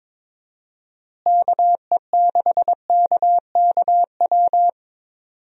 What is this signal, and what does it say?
Morse code, heard as a keyed tone, about 22 wpm, 700 Hz. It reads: KE6KKW
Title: USA Call Signs - 6 Characters - Repeated 3x - 22wpm